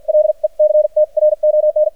Morse code, heard as a keyed tone, about 48 wpm, spaced as for 41 wpm, 610 Hz. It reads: LECTRON
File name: archived / CRSnet